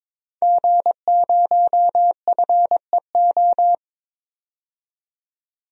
Morse code, heard as a keyed tone, about 21 wpm, 700 Hz. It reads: G0FEO
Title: DXCC-100 Call Signs - 5 Characters - 22wpm